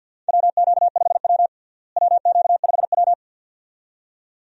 Morse code, heard as a keyed tone, about 52 wpm, 700 Hz. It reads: WQ5W WQ5W
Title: USA Call Signs - 4 Characters - Repeated 2x - 50wpm